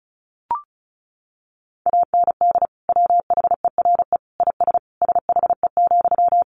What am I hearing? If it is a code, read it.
Morse code, heard as a keyed tone, about 34 wpm, 700 Hz. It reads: AND WHERE IS SHE,
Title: Sentences from Top 200 Words 35wpm